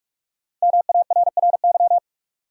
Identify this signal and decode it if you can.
Morse code, heard as a keyed tone, about 42 wpm, 700 Hz. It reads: MARRY
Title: Sentences from Top 1200 Words - New Word Drill - 45wpm